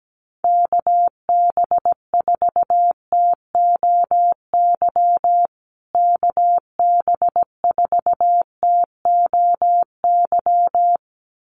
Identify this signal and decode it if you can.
Morse code, heard as a keyed tone, about 17 wpm, 700 Hz. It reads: KB4TOY KB4TOY
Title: USA Call Signs Repeated 2x 17wpm